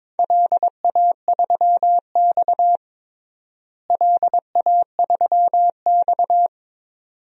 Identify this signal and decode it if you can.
Morse code, heard as a keyed tone, about 22 wpm, 700 Hz. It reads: LA3X LA3X
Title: DXCC-100 Call Signs - 4 Characters - Repeated 2x - 22wpm